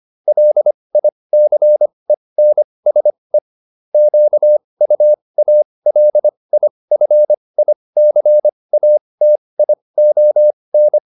Morse code, heard as a keyed tone, about 25 wpm, spaced as for 20 wpm, 600 Hz. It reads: LICENSE QUALIFICATION